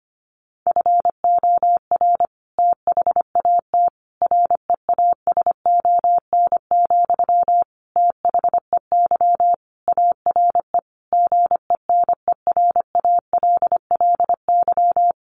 Morse code, heard as a keyed tone, about 25 wpm, 700 Hz. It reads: FOR THAT REASON, THEY ARE GENERALLY
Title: Sentences from Top 1500 Words 25wpm